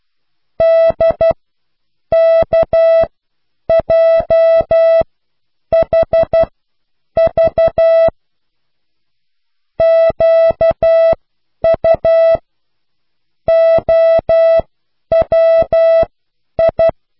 Morse code, heard as a keyed tone, about 12 wpm, spaced as for 8 wpm, 660 Hz. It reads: DKJHV QUOWI